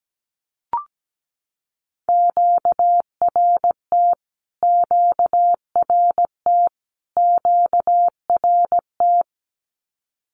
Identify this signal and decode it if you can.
Morse code, heard as a keyed tone, about 17 wpm, 700 Hz. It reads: QRT QRT QRT
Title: Q-Codes Repeated 3x 17wpm